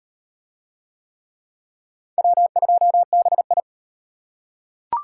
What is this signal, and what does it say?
Morse code, heard as a keyed tone, about 33 wpm, 700 Hz. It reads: W2BI